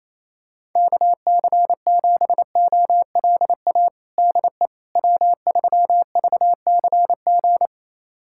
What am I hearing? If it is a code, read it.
Morse code, heard as a keyed tone, about 28 wpm, 700 Hz. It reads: KC7OLA DE W3VCG
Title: USA Callsign DE Callsign 28wpm